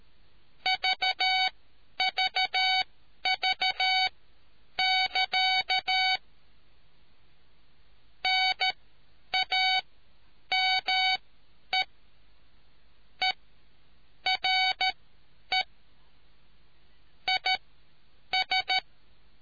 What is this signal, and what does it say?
Morse code, heard as a keyed tone, about 13 wpm, spaced as for 9 wpm, 750 Hz. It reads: VVV<KA> NAME ERE IS